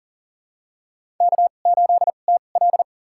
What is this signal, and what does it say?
Morse code, heard as a keyed tone, about 40 wpm, spaced as for 29 wpm, 700 Hz. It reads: K8TL
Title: USA Call Signs - 4 Characters - 2x Character Spacing - 40wpm